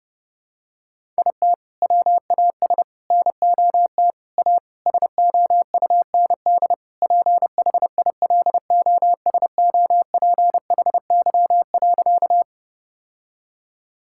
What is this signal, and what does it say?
Morse code, heard as a keyed tone, about 30 wpm, 700 Hz. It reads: IT WAS NOT A SOUND PHILOSOPHY.
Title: Sentences from Top 1000 Words - Review Part 6 - 30wpm